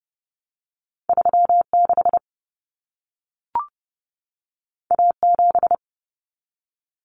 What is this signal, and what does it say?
Morse code, heard as a keyed tone, about 30 wpm, 700 Hz. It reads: 36 A7
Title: Common Cut Numbers - 2-Digit 30wpm